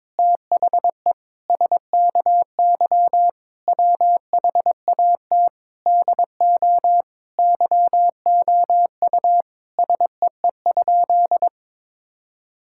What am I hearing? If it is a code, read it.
Morse code, heard as a keyed tone, about 22 wpm, 700 Hz. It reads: THE SKY WHAT DO YOU SEE?
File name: practice / 22wpm / Sentences with 3-Word Phrases - 22wpm